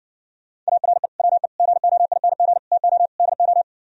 Morse code, heard as a keyed tone, about 70 wpm, 700 Hz. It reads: ARE WE GOING TO DO